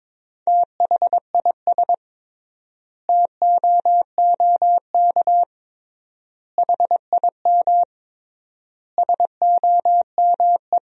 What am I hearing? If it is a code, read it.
Morse code, heard as a keyed tone, about 22 wpm, 700 Hz. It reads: THIS TOOK HIM SOME
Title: 3x Word-Spacing - Sentences from Top 300 Words - 22wpm